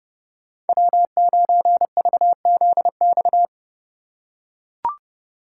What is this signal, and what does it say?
Morse code, heard as a keyed tone, about 32 wpm, 700 Hz. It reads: W9VZX
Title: USA Call Signs - 5 Characters - 4x Character Spacing - 30wpm